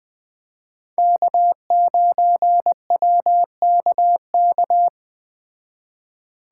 Morse code, heard as a keyed tone, about 20 wpm, 700 Hz. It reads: K9WKK
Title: Speed Racing - USA Call Signs 17wpm